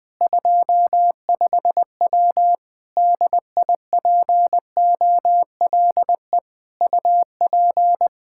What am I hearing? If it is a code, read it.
Morse code, heard as a keyed tone, about 20 wpm, 700 Hz. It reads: 25W DIPOLE UP
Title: K1USN SST QSO Practice 20wpm